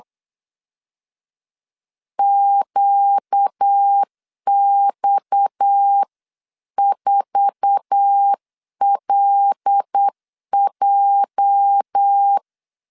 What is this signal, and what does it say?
Morse code, heard as a keyed tone, about 8 wpm, 800 Hz. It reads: QX4LJ
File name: CW / DET / 1 9 1500C S C 8wpm 5653